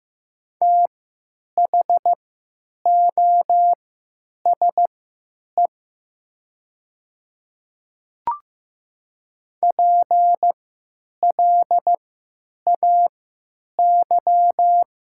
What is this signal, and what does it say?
Morse code, heard as a keyed tone, about 15 wpm, spaced as for 9 wpm, 700 Hz. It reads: THOSE PLAY
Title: 3x Character Spacing - Top 500 Words - 15wpm